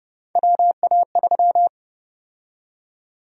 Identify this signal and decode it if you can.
Morse code, heard as a keyed tone, about 29 wpm, 700 Hz. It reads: WA3
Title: USA Call Sign Prefixes - 30wpm